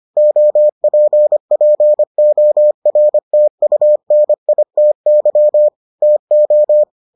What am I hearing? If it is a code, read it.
Morse code, heard as a keyed tone, about 25 wpm, 600 Hz. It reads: OPPORTUNITY TO